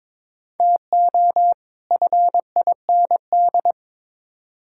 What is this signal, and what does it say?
Morse code, heard as a keyed tone, about 22 wpm, 700 Hz. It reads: TO FIND
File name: Build A Phrase - 2-Word Phrases - 22wpm